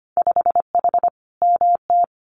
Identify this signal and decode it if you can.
Morse code, heard as a keyed tone, about 25 wpm, 700 Hz. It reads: 5H MT
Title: Winter Field Day Exchanges Encore 25wpm